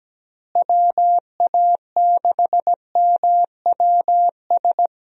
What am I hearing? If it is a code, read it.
Morse code, heard as a keyed tone, about 17 wpm, 700 Hz. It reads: WA6MWS
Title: USA Call Signs - 6 Characters - Repeated 3x - 17wpm